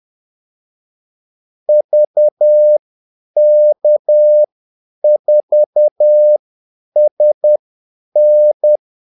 Morse code, heard as a keyed tone, about 10 wpm, spaced as for 8 wpm, 600 Hz. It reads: VK4SN
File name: Callsigns-10-8wpm